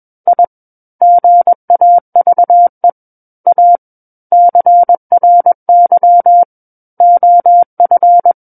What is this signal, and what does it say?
Morse code, heard as a keyed tone, about 21 wpm, 700 Hz. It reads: I GAVE A CRY OF
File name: Book One / 12-Chapter 12 - 21 WPM